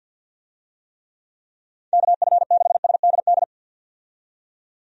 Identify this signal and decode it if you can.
Morse code, heard as a keyed tone, about 52 wpm, 700 Hz. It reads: KF6SDD
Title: USA Call Signs - 6 Characters - 50wpm